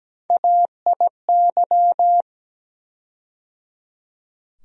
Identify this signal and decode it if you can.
Morse code, heard as a keyed tone, about 17 wpm, 700 Hz. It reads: AIY
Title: USA Call Sign Postfixes - 3 Letters 17wpm